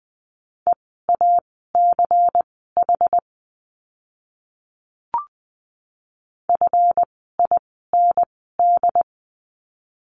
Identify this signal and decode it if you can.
Morse code, heard as a keyed tone, about 20 wpm, spaced as for 15 wpm, 700 Hz. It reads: EACH FIND
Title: 192 Lesson 31f - Letter v - Words 20wpm